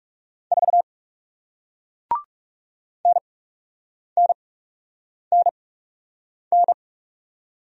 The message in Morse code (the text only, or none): 4 NNNN